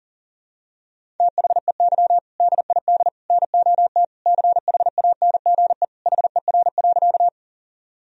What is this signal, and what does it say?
Morse code, heard as a keyed tone, about 40 wpm, 700 Hz. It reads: THEY DID NOT CHANGE HER.
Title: Sentences from Top 1000 Words - Review Part 1 - 40wpm